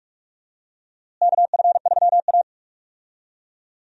Morse code, heard as a keyed tone, about 46 wpm, 700 Hz. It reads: KF3A
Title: USA Call Signs - 4 Characters - Repeated 2x - 45wpm